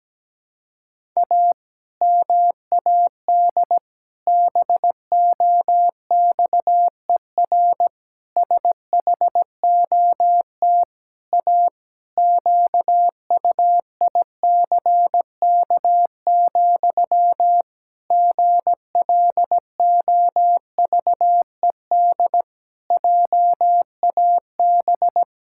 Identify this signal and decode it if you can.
Morse code, heard as a keyed tone, about 17 wpm, 700 Hz. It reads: A MAD BOXER SHOT A QUICK, GLOVED JAB
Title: Pangram Challenge v2 - 17wpm